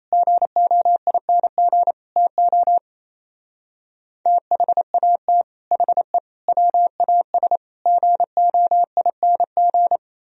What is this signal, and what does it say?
Morse code, heard as a keyed tone, about 35 wpm, 700 Hz. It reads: GOING TO THAT HE WAS GOING
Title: Speed-Racing - Sets of 5 Words 28wpm